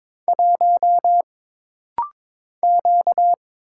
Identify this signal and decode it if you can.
Morse code, heard as a keyed tone, about 22 wpm, 700 Hz. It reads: 1 Q
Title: Single Letter-Number - Rapid-Fire 22wpm